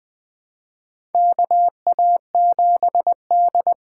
Morse code, heard as a keyed tone, about 20 wpm, 700 Hz. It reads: KA7D
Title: USA Call Signs - 6 Characters - 6x Character Spacing - 20wpm